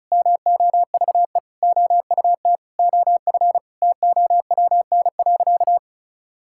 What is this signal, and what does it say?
Morse code, heard as a keyed tone, about 34 wpm, 700 Hz. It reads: MOVE OUT OF TOWN.
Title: Sentences from Top 1000 Words - Review Part 6 - 35wpm